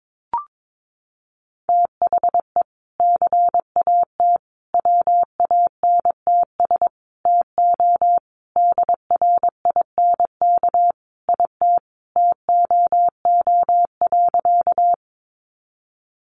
Morse code, heard as a keyed tone, about 22 wpm, 700 Hz. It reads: THE CAT WANTS TO DRINK IT TOO.
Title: Additional Easy Phrases 22wpm